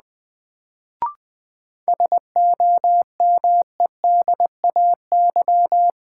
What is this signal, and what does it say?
Morse code, heard as a keyed tone, about 20 wpm, 700 Hz. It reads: SOMEDAY
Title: Sentences from Top 1600 Words - New Word Drill - 20wpm